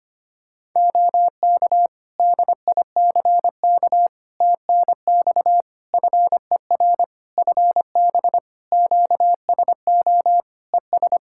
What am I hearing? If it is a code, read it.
Morse code, heard as a keyed tone, about 25 wpm, 700 Hz. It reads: OK DICK TNX FER FB QSO ES
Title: Practice QSO - Exchange 4 - Ending - 25wpm